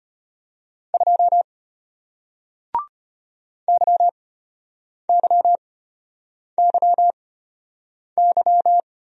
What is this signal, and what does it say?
Morse code, heard as a keyed tone, about 34 wpm, spaced as for 8 wpm, 700 Hz. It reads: 2 YYYY